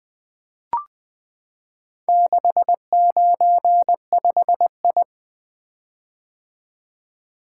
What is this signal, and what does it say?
Morse code, heard as a keyed tone, about 20 wpm, 700 Hz. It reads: 695I
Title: Four Letter-Number Combinations - 20wpm